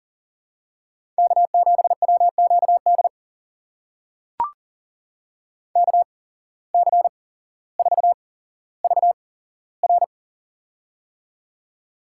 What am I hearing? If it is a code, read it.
Morse code, heard as a keyed tone, about 38 wpm, spaced as for 17 wpm, 700 Hz. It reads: K7WQD KC4VR
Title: USA Call Signs - 5 Characters - 8x Character Spacing - 40wpm